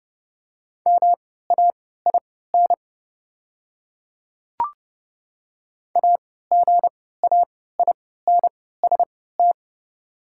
Morse code, heard as a keyed tone, about 30 wpm, spaced as for 17 wpm, 690 Hz. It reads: MAIN AGAINST
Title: 3x Character Spacing - Top 500 Words - 30wpm